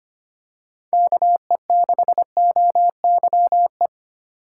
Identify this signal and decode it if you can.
Morse code, heard as a keyed tone, about 25 wpm, 700 Hz. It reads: KE6OYE
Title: USA Call Signs - 6 Characters - 4x Character Spacing - 25wpm